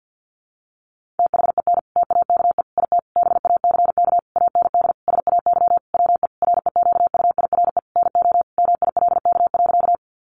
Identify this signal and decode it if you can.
Morse code, heard as a keyed tone, about 50 wpm, 700 Hz. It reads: THEN TAKE IT BACK AND SAY WE REQUIRE NO MILK.
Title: Sentences from Top 1500 Words 50wpm